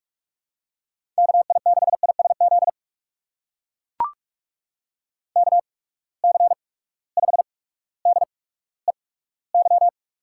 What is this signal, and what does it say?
Morse code, heard as a keyed tone, about 44 wpm, spaced as for 19 wpm, 700 Hz. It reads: KI6ISZ KC5DEY
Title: USA Call Signs - 6 Characters - 8x Character Spacing - 45wpm